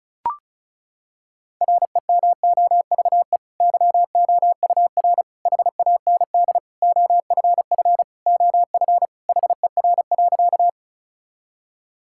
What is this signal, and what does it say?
Morse code, heard as a keyed tone, about 36 wpm, 700 Hz. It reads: REMOVE YOUR HAND OFF OF HER.
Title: Sentences from Top 1000 Words - Review Part 5 - 35wpm